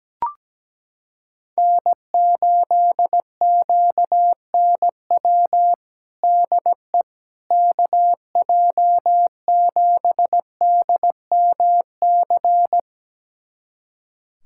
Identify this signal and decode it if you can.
Morse code, heard as a keyed tone, about 17 wpm, 700 Hz. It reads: N8QNW DE KJ7DMC
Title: USA Callsign DE Callsign 17wpm